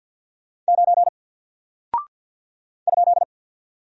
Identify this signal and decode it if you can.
Morse code, heard as a keyed tone, about 48 wpm, 700 Hz. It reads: 9 ?
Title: Single Character (Letter:Number:Punctuation:Prosign) Alt 50wpm